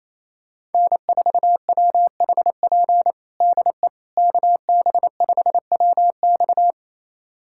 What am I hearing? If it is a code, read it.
Morse code, heard as a keyed tone, about 28 wpm, 700 Hz. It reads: N4WHP DE KB5WX
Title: USA Callsign DE Callsign - 5 Characters - 28wpm